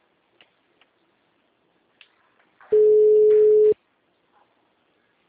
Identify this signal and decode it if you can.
Ringback tone (Europe, 425 Hz).
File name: out- -2010- - - .62